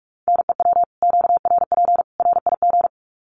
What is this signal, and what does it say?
Morse code, heard as a keyed tone, about 44 wpm, 700 Hz. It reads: NEW QRP RIG
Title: Common QSO Phrases 45wpm